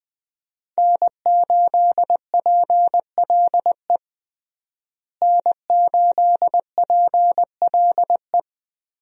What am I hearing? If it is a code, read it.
Morse code, heard as a keyed tone, about 21 wpm, 700 Hz. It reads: N8PLE N8PLE
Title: USA Call Signs - 5 Characters - Repeated 3x - 20wpm